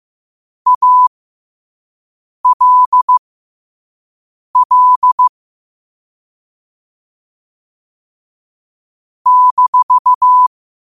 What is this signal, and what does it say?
Morse code, heard as a keyed tone, about 15 wpm, spaced as for 5 wpm, 1000 Hz. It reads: ALL -
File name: qotdfiles / 5wpm